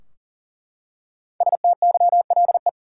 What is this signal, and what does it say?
Morse code, heard as a keyed tone, about 40 wpm, 700 Hz. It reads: STYLE
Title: Five Letter Words - 40wpm